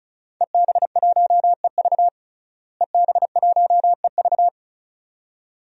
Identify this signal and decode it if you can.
Morse code, heard as a keyed tone, about 36 wpm, 700 Hz. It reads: EB1EV EB1EV
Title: DXCC-100 Call Signs - 5 Characters - Repeated 2x - 35wpm